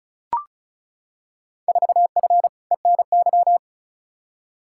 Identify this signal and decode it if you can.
Morse code, heard as a keyed tone, about 36 wpm, 700 Hz. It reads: 4F ENY
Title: Field Day Exchanges - 35wpm